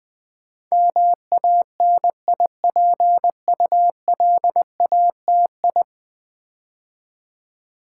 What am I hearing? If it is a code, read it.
Morse code, heard as a keyed tone, about 20 wpm, 700 Hz. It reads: MANIPULATI
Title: Build A Word - 5 Letters At A Time - 12-Letter Words - 20wpm